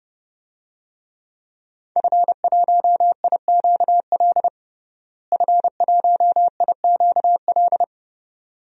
Morse code, heard as a keyed tone, about 30 wpm, 700 Hz. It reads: F1IQL F1IQL
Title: DXCC-100 Call Signs - 5 Characters - Repeated 2x - 30wpm